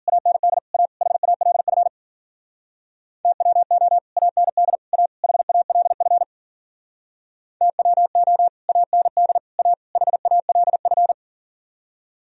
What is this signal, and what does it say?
Morse code, heard as a keyed tone, about 54 wpm, 700 Hz. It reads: AND A HALF TWO AND A HALF TWO AND A HALF